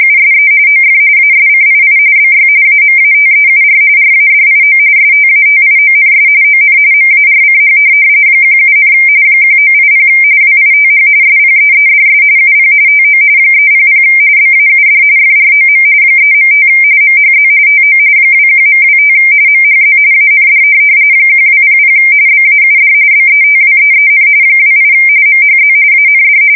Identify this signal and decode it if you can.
RTTY, about 45.45 baud. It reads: USE OF ASCII BY AMATEUR RADIO STATIONS ON MARCH 17, 1980 WITH SPEEDS UP TO 300 BAUD FROM 3.5 MHZ TO 21.25 MHZ AND 1200 BAUD BETWEEN 28 MHZ AND
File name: fsk170 / teletype-RTTY